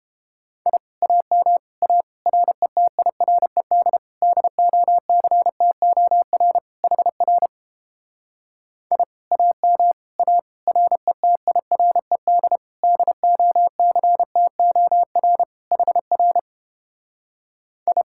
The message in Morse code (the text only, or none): I AM A RETIRED DOCTOR HR I AM A RETIRED DOCTOR HR I